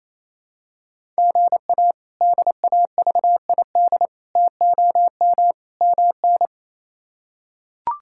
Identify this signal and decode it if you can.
Morse code, heard as a keyed tone, about 28 wpm, 700 Hz. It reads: GA DAVID TOM MN
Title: K1USN SST QSO Practice - 28wpm